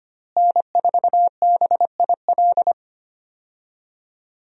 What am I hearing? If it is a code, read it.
Morse code, heard as a keyed tone, about 25 wpm, 700 Hz. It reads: N4BIL
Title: USA Call Signs - 5 Characters - Repeated 2x - 25wpm